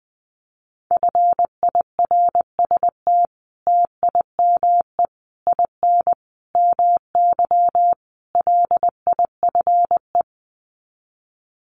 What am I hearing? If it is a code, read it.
Morse code, heard as a keyed tone, about 20 wpm, 700 Hz. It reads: FIRST TIME IN MY LIFE
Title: Sets of 5 Words Encore2 20wpm